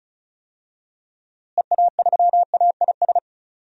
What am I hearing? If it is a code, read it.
Morse code, heard as a keyed tone, about 36 wpm, 700 Hz. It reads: EA3AIS
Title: DXCC-100 Call Signs - 6 Characters - Repeated 2x - 35wpm